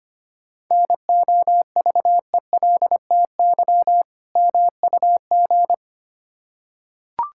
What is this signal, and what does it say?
Morse code, heard as a keyed tone, about 25 wpm, 700 Hz. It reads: NOVELTY MUG
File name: practice / 25wpm / Difficult Words with Hints - 25wpm